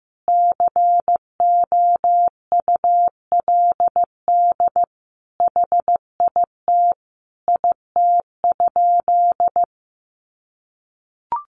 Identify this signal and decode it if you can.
Morse code, heard as a keyed tone, about 15 wpm, 700 Hz. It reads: COULD HIT IT?
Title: Sentences from Top 1200 Words 15wpm